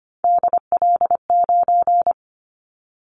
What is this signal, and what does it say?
Morse code, heard as a keyed tone, about 25 wpm, 700 Hz. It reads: DL9
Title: Callsign Prefixes for Top 25 DXCC - 25wpm